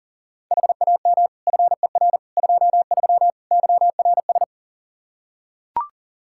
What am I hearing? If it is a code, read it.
Morse code, heard as a keyed tone, about 40 wpm, 700 Hz. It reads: HAM FER 23 YRS
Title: Practice QSO - Exchange 3 - More Info - 40wpm